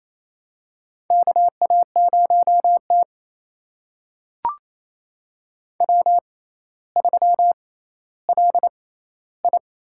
Morse code, heard as a keyed tone, about 28 wpm, spaced as for 14 wpm, 700 Hz. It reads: KA0T W3LI
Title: USA Call Signs - 4 Characters - 6x Character Spacing - 28wpm